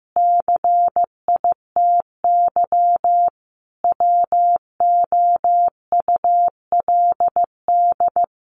Morse code, heard as a keyed tone, about 15 wpm, 700 Hz. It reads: CITY WOULD
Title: Sentences from Top 1100 Words 15wpm